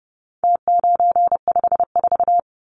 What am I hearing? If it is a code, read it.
Morse code, heard as a keyed tone, about 30 wpm, 700 Hz. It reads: T954
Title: Numbers - Contest Style - 4-Digit 30wpm